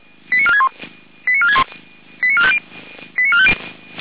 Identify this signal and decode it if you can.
Five-tone selective call (ZVEI), heard as quick runs of five tones, tones of about 70 ms. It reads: 89461 89461 89450 89450
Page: SDR/ZVEI